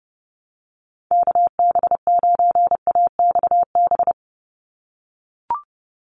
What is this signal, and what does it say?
Morse code, heard as a keyed tone, about 30 wpm, 700 Hz. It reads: KB9AXB
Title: USA Call Signs 30wpm